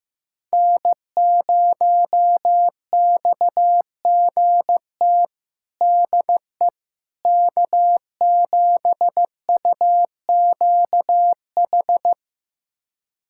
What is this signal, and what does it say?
Morse code, heard as a keyed tone, about 15 wpm, 700 Hz. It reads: N0XGT DE K7UQH
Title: USA Callsign DE Callsign - 5 Characters - 15wpm